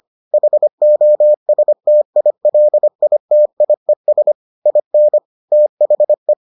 Morse code, heard as a keyed tone, about 25 wpm, 600 Hz. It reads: HOSTILITIES IN THE